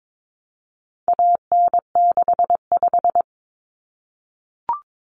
Morse code, heard as a keyed tone, about 22 wpm, 700 Hz. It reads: AN65
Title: Common Cut Numbers - 4-Digit 22wpm